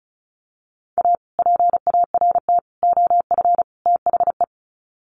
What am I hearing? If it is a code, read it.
Morse code, heard as a keyed tone, about 34 wpm, 700 Hz. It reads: A PART OF THE
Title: Sets of 4 Words 35wpm